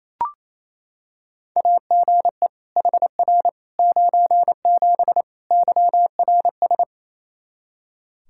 Morse code, heard as a keyed tone, about 27 wpm, 700 Hz. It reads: AGE HR 97 YRS
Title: Practice QSO - Exchange 3 - More Info - 28wpm